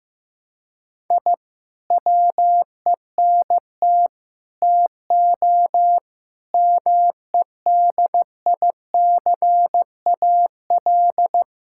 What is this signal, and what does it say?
Morse code, heard as a keyed tone, about 16 wpm, 700 Hz. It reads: I WENT TO MEDICAL
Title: Sentences from Top 1000 Words - Review Part 5 - 15wpm